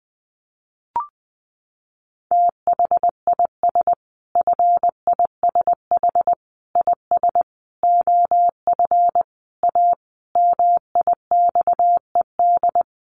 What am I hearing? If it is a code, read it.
Morse code, heard as a keyed tone, about 20 wpm, 700 Hz. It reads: THIS FISH IS OF A MIXED
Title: Sentences from Top 1500 Words 20wpm